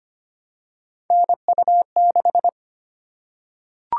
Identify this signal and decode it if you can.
Morse code, heard as a keyed tone, about 25 wpm, 700 Hz. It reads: NU6
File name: Three Letter-Number Combinations - 25wpm